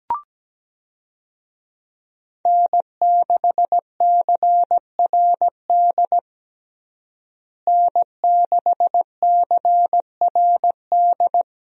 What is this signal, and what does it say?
Morse code, heard as a keyed tone, about 17 wpm, 700 Hz. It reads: N6CRD N6CRD
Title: USA Call Signs - 5 Characters - Repeated 3x - 17wpm